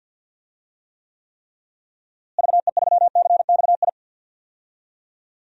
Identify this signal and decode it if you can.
Morse code, heard as a keyed tone, about 46 wpm, 700 Hz. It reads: VE3CXI
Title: DXCC-100 Call Signs - 6 Characters - Repeated 3x - 50wpm